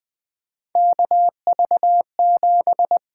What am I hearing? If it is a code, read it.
Morse code, heard as a keyed tone, about 20 wpm, 700 Hz. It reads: KV7
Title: USA Call Signs - 4 Characters - 8x Character Spacing - 20wpm